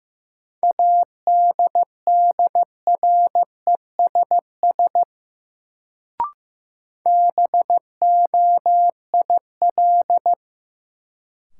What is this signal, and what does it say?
Morse code, heard as a keyed tone, about 15 wpm, 700 Hz. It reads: ADDRESS BOIL
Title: Sets of 5 Words Encore - New Word Drill - 15wpm